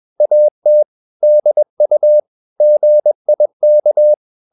Morse code, heard as a keyed tone, about 21 wpm, 600 Hz. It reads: AT DU GIK
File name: Ferie QRS 21wpm